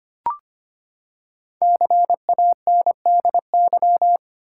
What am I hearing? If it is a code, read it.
Morse code, heard as a keyed tone, about 25 wpm, 700 Hz. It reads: CANDY
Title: Five Letter Words - 25wpm